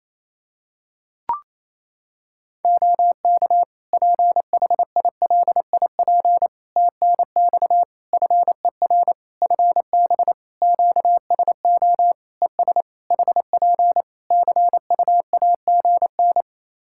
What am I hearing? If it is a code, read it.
Morse code, heard as a keyed tone, about 28 wpm, 700 Hz. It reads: OK PHILIP TNX FER FB QSO ES HP CUAGN
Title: Practice QSO - Exchange 4 - Ending - 28wpm